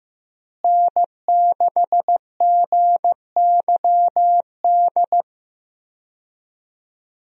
Morse code, heard as a keyed tone, about 15 wpm, 700 Hz. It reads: N6GYD
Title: USA Call Signs - 5 Characters - Repeated 2x - 15wpm